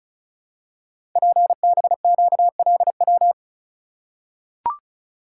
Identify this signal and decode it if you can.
Morse code, heard as a keyed tone, about 36 wpm, 690 Hz. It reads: PBQLW
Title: Code Groups - 5 Letters - 35wpm